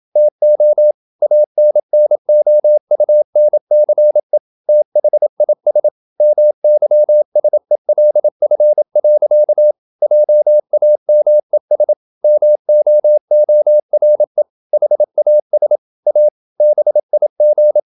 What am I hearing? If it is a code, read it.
Morse code, heard as a keyed tone, about 27 wpm, 600 Hz. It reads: TO ANNOUNCE THIS MYSELF. JAMES MOORE HAS A BIG
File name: Never 0027wpm